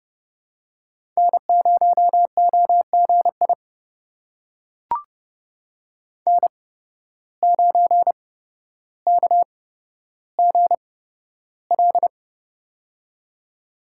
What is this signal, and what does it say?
Morse code, heard as a keyed tone, about 29 wpm, spaced as for 13 wpm, 700 Hz. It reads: N0OGI N9KGL
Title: USA Call Signs - 5 Characters - 8x Character Spacing - 30wpm